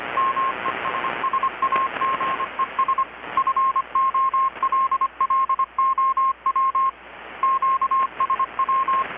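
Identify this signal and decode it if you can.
Morse code, heard as a keyed tone, about 25 wpm, 1070 Hz, with radio noise. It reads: MESSAGES FOLLOW QSW
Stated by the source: Morse from KPH's Night of Nights XX maritime radiotelegraphy event, received in Chiapas, Mexico about 3,800 km away on a Kenwood R-600 with a grounded 10 m long-wire antenna.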